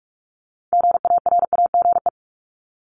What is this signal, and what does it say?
Morse code, heard as a keyed tone, about 46 wpm, 700 Hz. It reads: GARAGE
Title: Multi-letter Phonemes Repeated 3x 45wpm